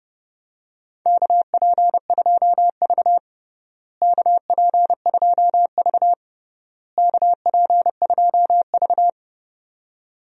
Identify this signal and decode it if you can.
Morse code, heard as a keyed tone, about 30 wpm, 700 Hz. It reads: KP2V KP2V KP2V
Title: DXCC-100 Call Signs - 4 Characters - Repeated 3x - 30wpm